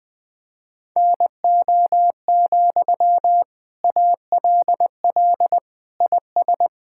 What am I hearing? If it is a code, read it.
Morse code, heard as a keyed tone, about 20 wpm, 700 Hz. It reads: NO, ALL IS
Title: Sentences from Top 1600 Words 20wpm